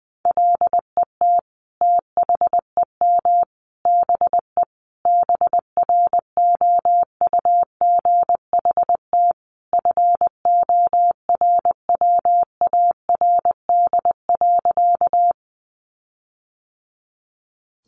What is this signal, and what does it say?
Morse code, heard as a keyed tone, about 20 wpm, 700 Hz. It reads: LET THEM BE BROUGHT FORWARD.
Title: Sentences from Top 1300 Words 20wpm